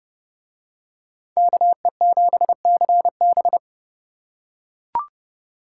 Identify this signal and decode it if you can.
Morse code, heard as a keyed tone, about 30 wpm, 700 Hz. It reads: KE7CB